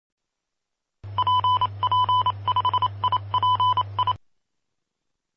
Morse code, heard as a keyed tone, about 27 wpm, 1000 Hz. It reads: PP5IPI
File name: pp5bar / aulas de telegrafia / Aula-04